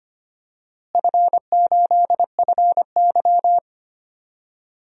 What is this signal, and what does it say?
Morse code, heard as a keyed tone, about 25 wpm, 700 Hz. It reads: F8FY
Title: DXCC-100 Call Signs - 4 Characters - Repeated 3x - 25wpm